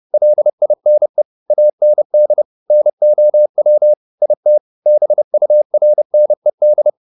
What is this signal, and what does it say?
Morse code, heard as a keyed tone, about 30 wpm, 600 Hz. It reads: LINE AND NOW IT BURNED